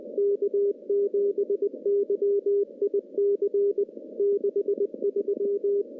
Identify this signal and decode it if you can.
Morse code, heard as a keyed tone, about 20 wpm, 400 Hz. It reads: K7YIC 63
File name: CW-SS-k5tr